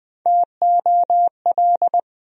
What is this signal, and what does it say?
Morse code, heard as a keyed tone, about 20 wpm, 700 Hz. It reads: TOL
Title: USA Call Sign Postfixes - 3 Letters 20wpm